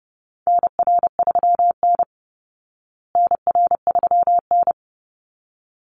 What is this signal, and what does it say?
Morse code, heard as a keyed tone, about 30 wpm, 700 Hz. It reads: NR3N NR3N
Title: Call Signs Repeated 3x - 30wpm